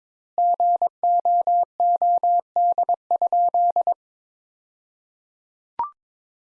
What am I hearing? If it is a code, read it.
Morse code, heard as a keyed tone, about 22 wpm, 700 Hz. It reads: GOOD?
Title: Sentences Top 200 Words - 22wpm